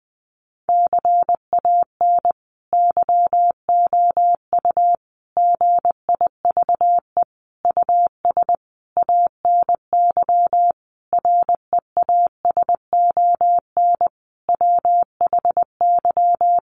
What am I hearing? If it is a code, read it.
Morse code, heard as a keyed tone, about 19 wpm, 700 Hz. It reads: CAN YOU GIVE US ANY REASON WHY
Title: Sentences from Top 1100 Words 20wpm